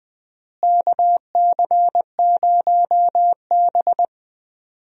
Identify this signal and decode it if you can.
Morse code, heard as a keyed tone, about 20 wpm, 700 Hz. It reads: KC0B
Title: USA Call Signs - 4 Characters - Repeated 2x - 20wpm